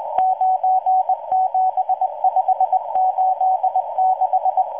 Morse code, heard as a keyed tone, about 20 wpm, 750 Hz. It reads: 97586